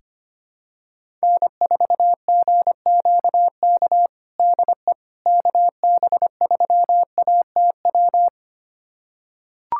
Morse code, heard as a keyed tone, about 25 wpm, 700 Hz. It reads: N4GQK DE KB3ATW
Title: USA Callsign DE Callsign 25wpm